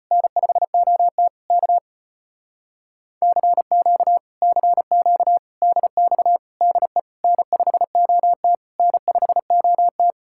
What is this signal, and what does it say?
Morse code, heard as a keyed tone, about 33 wpm, 700 Hz. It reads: N5OT K CQ CQ DX DE N5OT N5OT